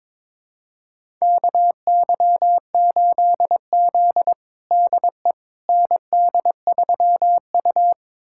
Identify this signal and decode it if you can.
Morse code, heard as a keyed tone, about 21 wpm, 700 Hz. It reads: KY8Z DE ND3U
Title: USA Callsign DE Callsign - 4 Characters - 22wpm